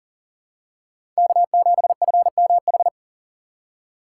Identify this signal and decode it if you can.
Morse code, heard as a keyed tone, about 40 wpm, 700 Hz. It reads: K7FMH
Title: USA Call Signs - 5 Characters - 6x Character Spacing - 40wpm